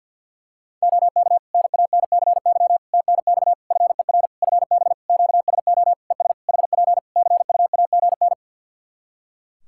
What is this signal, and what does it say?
Morse code, heard as a keyed tone, about 53 wpm, 700 Hz. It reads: OK NANCY TNX FER FB QSO ES HP CUAGN
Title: Practice QSO - Exchange 4 - Ending - 50wpm